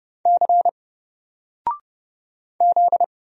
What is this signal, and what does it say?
Morse code, heard as a keyed tone, about 31 wpm, 700 Hz. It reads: C Z
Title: Single Character (Letter:Number:Punctuation:Prosign) 30wpm